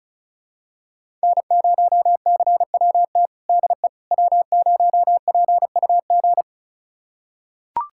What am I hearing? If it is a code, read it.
Morse code, heard as a keyed tone, about 34 wpm, 700 Hz. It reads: N0CWT DE W0PUG
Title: USA Callsign DE Callsign - 5 Characters - 35wpm